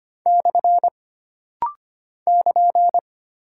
Morse code, heard as a keyed tone, about 25 wpm, 700 Hz. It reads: / <KN>
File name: Single Punctuation-Prosign - Rapid-Fire - 25wpm